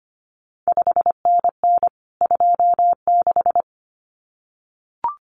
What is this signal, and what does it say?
Morse code, heard as a keyed tone, about 25 wpm, 700 Hz. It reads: 5NN 26
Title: CQ WW DX Contest Exchange Encore - 25wpm